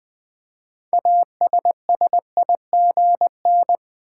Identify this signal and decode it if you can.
Morse code, heard as a keyed tone, about 20 wpm, 700 Hz. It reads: ASSIGN
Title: Build A Word - 3 Letters At A Time - 6-Letter Words - 20wpm